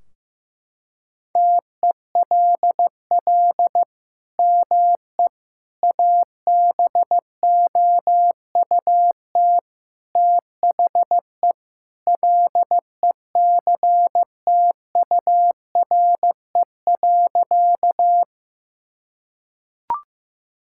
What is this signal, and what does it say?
Morse code, heard as a keyed tone, about 15 wpm, 700 Hz. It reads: TELL ME ABOUT THE LECTURE.
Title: Sentences from Top 1600 Words 15wpm